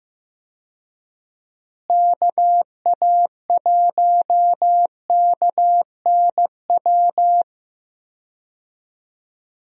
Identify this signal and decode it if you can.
Morse code, heard as a keyed tone, about 15 wpm, 700 Hz. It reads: KA1KNW
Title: USA Call Signs - 6 Characters - Repeated 2x - 15wpm